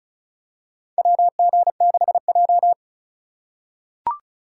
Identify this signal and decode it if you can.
Morse code, heard as a keyed tone, about 34 wpm, 700 Hz. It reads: WG6J
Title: USA Call Signs - 4x Character Spacing - 35wpm